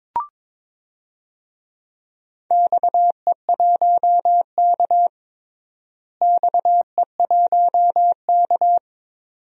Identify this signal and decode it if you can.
Morse code, heard as a keyed tone, about 22 wpm, 700 Hz. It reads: XE1K XE1K
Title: DXCC-100 Call Signs - 4 Characters - Repeated 2x - 22wpm